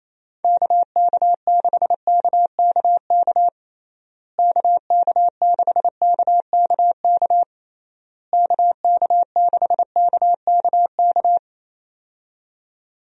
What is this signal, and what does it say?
Morse code, heard as a keyed tone, about 28 wpm, 700 Hz. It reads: KK6KKK KK6KKK KK6KKK
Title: USA Call Signs - 6 Characters - Repeated 3x - 28wpm